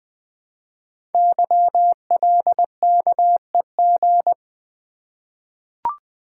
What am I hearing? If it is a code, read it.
Morse code, heard as a keyed tone, about 20 wpm, 700 Hz. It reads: YLKEG
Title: Code Groups - 5 Letters - 20wpm